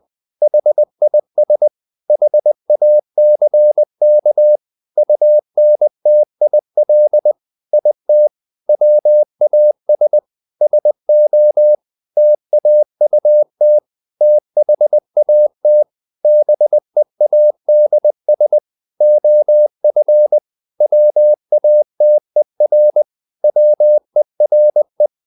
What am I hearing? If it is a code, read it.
Morse code, heard as a keyed tone, about 20 wpm, 600 Hz. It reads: HIS HACK UNTIL IT WAS SO TAUT THAT BEADS OF WATER WERE
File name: the old man and the sea / the old man and the sea part 1 20wpm